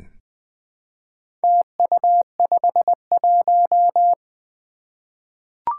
Morse code, heard as a keyed tone, about 20 wpm, 700 Hz. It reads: TU51
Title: Four Challenging Letter-Number Combinations 20wpm